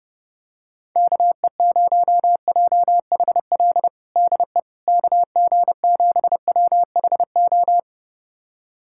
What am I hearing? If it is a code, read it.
Morse code, heard as a keyed tone, about 32 wpm, 700 Hz. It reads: KE0JHL DE KG7WHO
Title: USA Callsign DE Callsign 30wpm